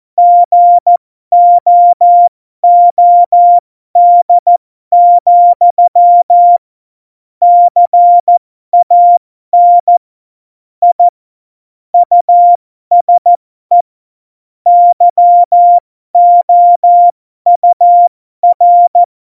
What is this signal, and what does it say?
Morse code, heard as a keyed tone, about 14 wpm, spaced as for 12 wpm, 700 Hz. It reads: GOOD, CAN I USE YOUR
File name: Morsekurs / Witze